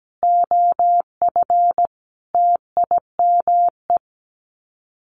Morse code, heard as a keyed tone, about 17 wpm, 700 Hz. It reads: OF TIME
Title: Sets of 5 Words Encore2 17wpm